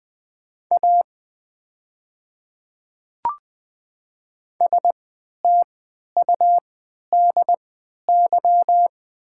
Morse code, heard as a keyed tone, about 20 wpm, spaced as for 11 wpm, 700 Hz. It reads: A STUDY
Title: 3x Character Spacing - Top 300 Words - 20wpm